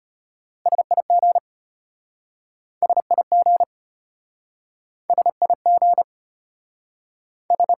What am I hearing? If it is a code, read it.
Morse code, heard as a keyed tone, about 33 wpm, 700 Hz. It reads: SIG SIG SIG S